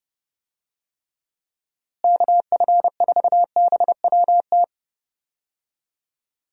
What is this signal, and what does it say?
Morse code, heard as a keyed tone, about 32 wpm, 700 Hz. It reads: KF4BWT
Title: USA Call Signs - 6 Characters - Repeated 3x - 30wpm